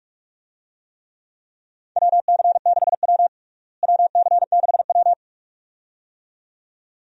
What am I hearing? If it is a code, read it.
Morse code, heard as a keyed tone, about 41 wpm, 700 Hz. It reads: WC6W WC6W
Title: USA Call Signs - 4 Characters - Repeated 2x - 45wpm